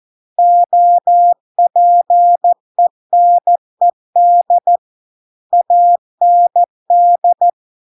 Morse code, heard as a keyed tone, about 14 wpm, 700 Hz. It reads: OPENED AND